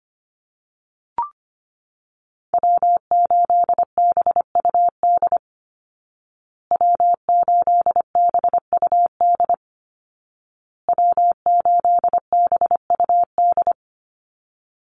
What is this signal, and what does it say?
Morse code, heard as a keyed tone, about 25 wpm, 700 Hz. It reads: W8BUD W8BUD W8BUD
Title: Call Signs Repeated 2x - 25wpm